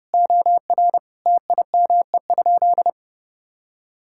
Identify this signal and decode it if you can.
Morse code, heard as a keyed tone, about 30 wpm, 700 Hz. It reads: OR TIME?
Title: Sentences from Top 1000 Words - Review Part 5 - 30wpm